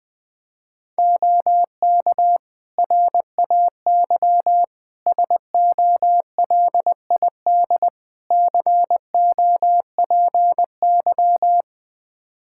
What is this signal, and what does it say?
Morse code, heard as a keyed tone, about 21 wpm, 700 Hz. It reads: OK RAY SOLID COPY
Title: Practice QSO - Exchange 3 - More Info - 20wpm